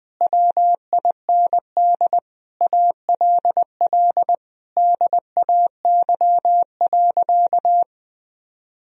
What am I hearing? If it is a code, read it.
Morse code, heard as a keyed tone, about 21 wpm, 700 Hz. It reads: WIND ALL DAY.
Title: Sentences from Top 1000 Words - Review Part 2 - 20wpm